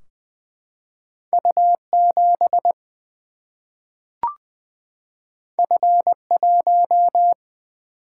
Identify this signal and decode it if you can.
Morse code, heard as a keyed tone, about 20 wpm, 700 Hz. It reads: U 7 F 1
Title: Single-Letter Single-Number 20wpm